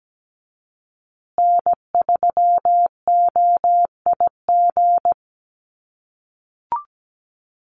Morse code, heard as a keyed tone, about 17 wpm, 700 Hz. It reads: N3OIG
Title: USA Call Signs Repeated 2x 17wpm